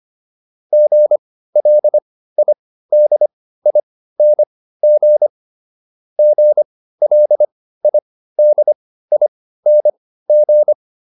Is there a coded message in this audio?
Yes: Morse code from 0.7 to 10.7 s, heard as a keyed tone, about 25 wpm, spaced as for 15 wpm, 600 Hz. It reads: GLIDING GLIDING